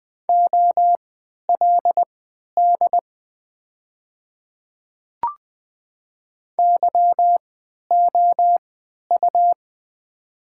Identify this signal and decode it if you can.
Morse code, heard as a keyed tone, about 20 wpm, spaced as for 11 wpm, 700 Hz. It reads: OLD YOU
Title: 3x Character Spacing - Top 500 Words - 20wpm